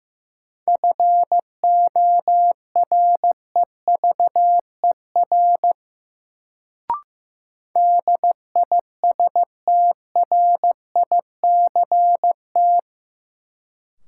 Morse code, heard as a keyed tone, about 15 wpm, 700 Hz. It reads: FOREVER DISTRICT
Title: Sentences from Top 1600 Words - New Word Drill - 15wpm